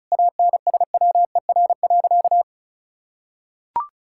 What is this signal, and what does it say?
Morse code, heard as a keyed tone, about 36 wpm, 700 Hz. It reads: ANSWER.
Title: Sentences from Top 1000 Words - Review Part 6 - 35wpm